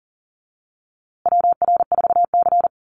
Morse code, heard as a keyed tone, about 40 wpm, 700 Hz. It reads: WR4C
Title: USA Call Signs Repeated 3x 40wpm